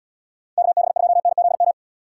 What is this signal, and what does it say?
Morse code, heard as a keyed tone, about 69 wpm, 700 Hz. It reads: KB2NZM